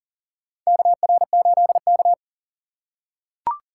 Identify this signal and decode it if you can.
Morse code, heard as a keyed tone, about 40 wpm, 700 Hz. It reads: KR8K
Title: USA Call Signs - 4 Characters - 2x Character Spacing - 40wpm